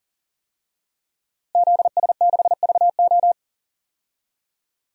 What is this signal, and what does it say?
Morse code, heard as a keyed tone, about 40 wpm, 700 Hz. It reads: ZS6VO
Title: DXCC-100 Call Signs - 5 Characters - Repeated 2x - 40wpm